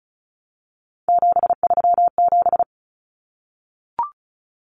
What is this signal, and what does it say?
Morse code, heard as a keyed tone, about 34 wpm, 700 Hz. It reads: 737
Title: Common Cut Numbers - 3-Digit 35wpm